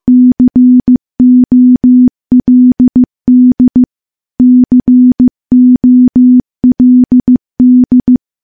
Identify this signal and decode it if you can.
Morse code, heard as a keyed tone, about 15 wpm, 250 Hz. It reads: COLD COLD
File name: morse-training-15WPM